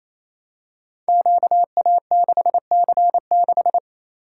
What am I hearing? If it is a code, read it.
Morse code, heard as a keyed tone, about 27 wpm, 700 Hz. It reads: QA6C6
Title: Code Groups - 5 Letter:Numbers - 28wpm